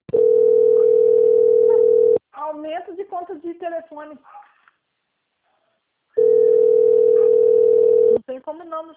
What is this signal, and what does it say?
Ringback tone (North America), 440 + 480 Hz, with speech.
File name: out- - - - - .34745